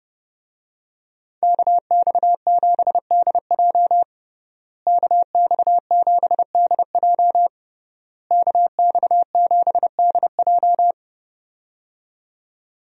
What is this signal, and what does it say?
Morse code, heard as a keyed tone, about 30 wpm, 700 Hz. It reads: KX7DJ KX7DJ KX7DJ
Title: USA Call Signs - 5 Characters - Repeated 3x - 30wpm